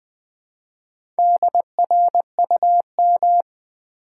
Morse code, heard as a keyed tone, about 20 wpm, 700 Hz. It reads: DRUM
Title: Build A Word - 2 Letters At A Time - 4-Letter Words - 20wpm